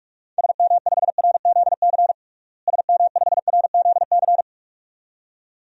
Morse code, heard as a keyed tone, about 46 wpm, 700 Hz. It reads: SM5RZC SM5RZC
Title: DXCC-100 Call Signs - 6 Characters - Repeated 2x - 45wpm